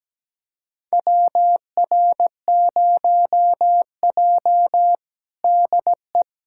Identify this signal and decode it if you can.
Morse code, heard as a keyed tone, about 17 wpm, 700 Hz. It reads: WR0J DE
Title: USA Callsign DE Callsign - 4 Characters - 17wpm